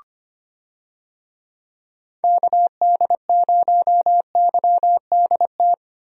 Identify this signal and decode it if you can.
Morse code, heard as a keyed tone, about 25 wpm, 700 Hz. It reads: KD0YDT
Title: USA Call Signs - 6 Characters - 25wpm